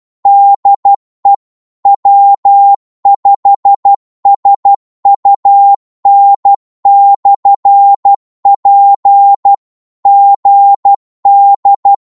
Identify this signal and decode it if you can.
Morse code, heard as a keyed tone, about 12 wpm, 800 Hz. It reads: DE W5SUN/P GD
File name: Morse / qso-12wpm